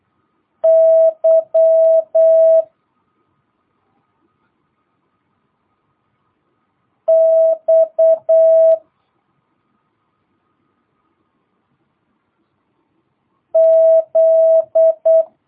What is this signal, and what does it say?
Morse code, heard as a keyed tone, about 8 wpm, 650 Hz. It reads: Y X Z
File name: jh8cbh / 2022cw